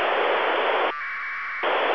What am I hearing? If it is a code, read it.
Packet radio (AX.25, as in APRS).